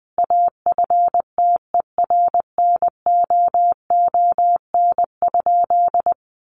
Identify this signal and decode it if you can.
Morse code, heard as a keyed tone, about 20 wpm, 700 Hz. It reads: AFTERNOON?
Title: Sentences from Top 1500 Words 20wpm